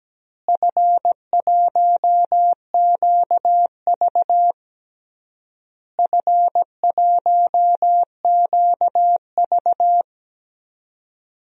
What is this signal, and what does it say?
Morse code, heard as a keyed tone, about 17 wpm, 700 Hz. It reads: F1QV F1QV
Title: DXCC-100 Call Signs - 4 Characters - Repeated 3x - 17wpm